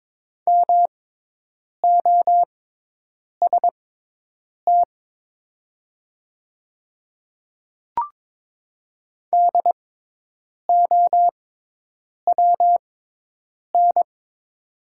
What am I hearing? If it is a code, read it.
Morse code, heard as a keyed tone, about 22 wpm, spaced as for 8 wpm, 700 Hz. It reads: MOST DOWN
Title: 6x Character Spacing - Top 300 Words - 22wpm